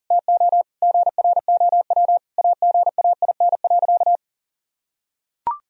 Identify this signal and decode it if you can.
Morse code, heard as a keyed tone, about 40 wpm, 700 Hz. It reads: TO GROW AGAIN.
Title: Sentences from Top 1000 Words - Review Part 1 - 40wpm